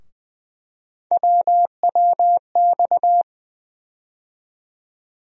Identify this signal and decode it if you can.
Morse code, heard as a keyed tone, about 20 wpm, 700 Hz. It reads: WWX
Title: USA Call Sign Postfixes - 3 Letters 20wpm